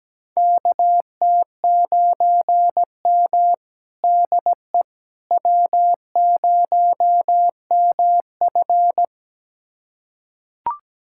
Morse code, heard as a keyed tone, about 17 wpm, 700 Hz. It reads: KT9M DE W0MF
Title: USA Callsign DE Callsign - 4 Characters - 17wpm